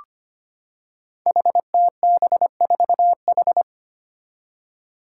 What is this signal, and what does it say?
Morse code, heard as a keyed tone, about 25 wpm, 700 Hz. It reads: HTB4H
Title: Five Challenging Letter-Number Combinations 25wpm